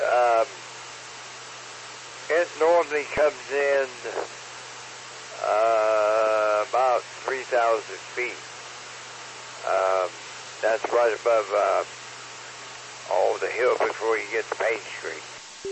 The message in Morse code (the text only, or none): none